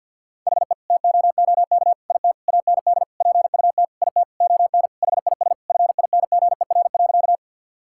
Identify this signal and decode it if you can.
Morse code, heard as a keyed tone, about 50 wpm, 700 Hz. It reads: HE TOOK IT AND PUT IT ON HIS FINGER.
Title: Sentences from Top 1600 Words 50wpm